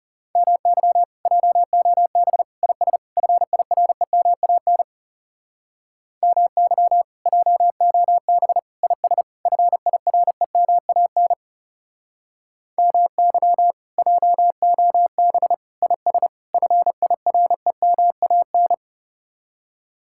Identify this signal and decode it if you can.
Morse code, heard as a keyed tone, about 39 wpm, 700 Hz. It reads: MY JOB IS FIREMAN MY JOB IS FIREMAN MY JOB IS FIREMAN